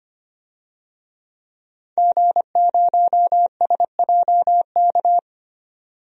Morse code, heard as a keyed tone, about 25 wpm, 690 Hz. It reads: G0SJK
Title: DXCC-100 Call Signs - 5 Characters - 25wpm